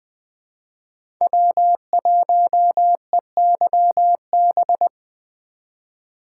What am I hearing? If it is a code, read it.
Morse code, heard as a keyed tone, about 20 wpm, 700 Hz. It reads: W1EYB
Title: USA Call Signs - 4x Character Spacing - 20wpm